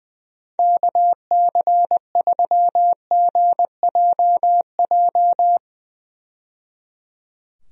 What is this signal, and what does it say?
Morse code, heard as a keyed tone, about 20 wpm, 700 Hz. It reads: KC3GJJ
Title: USA Call Signs - 6 Characters - 20wpm